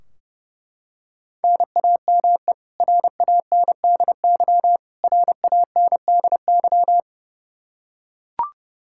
Morse code, heard as a keyed tone, about 32 wpm, 700 Hz. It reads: NAME RANDY RANDY
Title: Practice QSO - Exchange 1 - Beginning - 30wpm